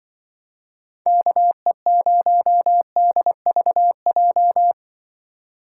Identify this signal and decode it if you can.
Morse code, heard as a keyed tone, about 23 wpm, 700 Hz. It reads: KE0DVJ
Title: Speed Racing - USA 2x3 Call Signs 20wpm